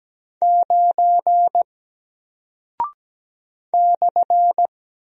Morse code, heard as a keyed tone, about 17 wpm, 700 Hz. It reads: 9 /
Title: Single Character (Letter:Number:Punctuation:Prosign) Alt 17wpm